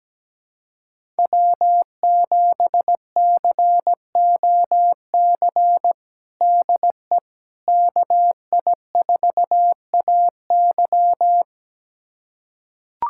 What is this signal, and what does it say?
Morse code, heard as a keyed tone, about 17 wpm, 700 Hz. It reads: W7COC DE KI4AY
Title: USA Callsign DE Callsign - 5 Characters - 17wpm